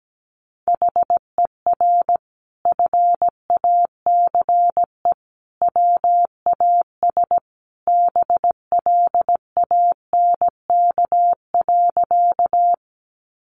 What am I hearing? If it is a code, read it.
Morse code, heard as a keyed tone, about 17 wpm, 700 Hz. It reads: HER FACE WAS BLANK.
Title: Sentences from Top 1500 Words 17wpm